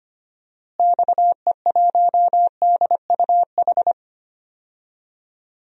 Morse code, heard as a keyed tone, about 25 wpm, 700 Hz. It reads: XE1DUH
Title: DXCC-100 Call Signs - 6 Characters - Repeated 2x - 25wpm